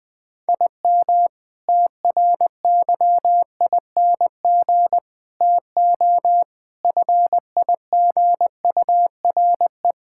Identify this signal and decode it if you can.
Morse code, heard as a keyed tone, about 21 wpm, 700 Hz. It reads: IM TRYING TO FIGURE
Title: Sentences with 6-Word Phrases - 20wpm